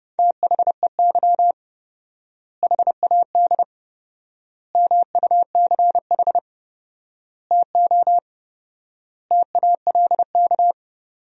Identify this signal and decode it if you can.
Morse code, heard as a keyed tone, about 30 wpm, 700 Hz. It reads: THEY HAD MUCH TO TALK
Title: 4x Word-Spacing - Sentences from Top 400 Words - 30wpm